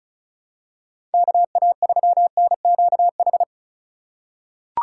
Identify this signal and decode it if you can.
Morse code, heard as a keyed tone, about 36 wpm, 700 Hz. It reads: KA3NQH
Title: USA Call Signs - 6 Characters - 4x Character Spacing - 35wpm